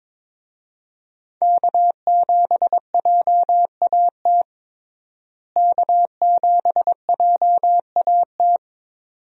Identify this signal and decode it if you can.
Morse code, heard as a keyed tone, about 22 wpm, 700 Hz. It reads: K7JAT K7JAT
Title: USA Call Signs - 5 Characters - Repeated 2x - 22wpm